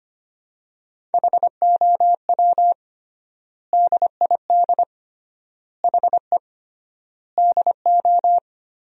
Morse code, heard as a keyed tone, about 25 wpm, 700 Hz. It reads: HOW DID HE DO
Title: 3x Word-Spacing - Sentences from Top 100 Words - 25wpm